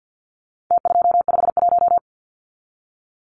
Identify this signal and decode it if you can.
Morse code, heard as a keyed tone, about 48 wpm, 700 Hz. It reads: T251
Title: Numbers - Contest Style - 4-Digit 50wpm